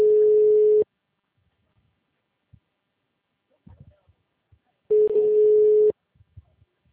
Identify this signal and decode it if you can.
Ringback tone (Europe, 425 Hz), with speech.